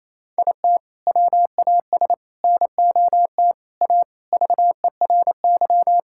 Morse code, heard as a keyed tone, about 28 wpm, 700 Hz. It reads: IT WAS NOT A VERY
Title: Sentences from Top 1000 Words - Review Part 6 - 28wpm